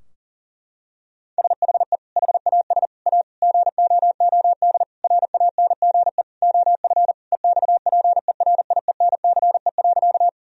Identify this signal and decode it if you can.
Morse code, heard as a keyed tone, about 40 wpm, 700 Hz. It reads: SHE HAS A GOOD RANGE OF EXPERIENCE.
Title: Sentences from Top 1000 Words - Review Part 4 - 40wpm